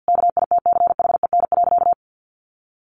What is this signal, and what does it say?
Morse code, heard as a keyed tone, about 48 wpm, 700 Hz. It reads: KITCHEN.
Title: Sentences from Top 1400 Words 50wpm